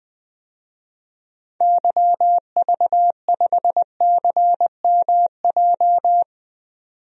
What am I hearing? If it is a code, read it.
Morse code, heard as a keyed tone, about 20 wpm, 700 Hz. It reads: YV5CMJ
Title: DXCC-100 Call Signs - 6 Characters - 20wpm